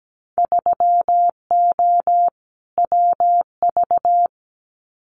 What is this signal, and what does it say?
Morse code, heard as a keyed tone, about 17 wpm, 700 Hz. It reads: 3O WV
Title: Winter Field Day Exchanges 17wpm